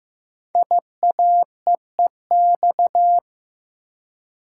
Morse code, heard as a keyed tone, about 15 wpm, 700 Hz. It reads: IAEEX
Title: Code Groups - 5 Letter:Numbers - 15wpm